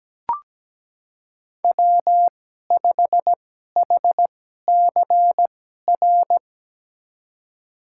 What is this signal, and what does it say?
Morse code, heard as a keyed tone, about 17 wpm, spaced as for 12 wpm, 700 Hz. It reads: W5HCR
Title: USA Call Signs - 5 Characters - 2x Character Spacing - 17wpm